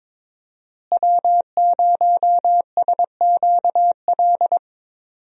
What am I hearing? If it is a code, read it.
Morse code, heard as a keyed tone, about 22 wpm, 700 Hz. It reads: W0SQL
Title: USA Call Signs - 5 Characters - Repeated 2x - 22wpm